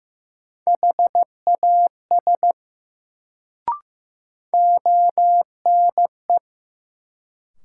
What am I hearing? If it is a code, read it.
Morse code, heard as a keyed tone, about 15 wpm, 700 Hz. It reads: HAS ONE
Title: Sentences from Top 100 Words - New Word Drill - 15wpm